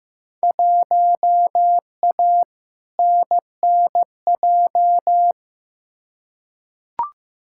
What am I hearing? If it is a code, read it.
Morse code, heard as a keyed tone, about 15 wpm, 700 Hz. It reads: 1A NNJ
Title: Field Day Exchanges Encore - 15wpm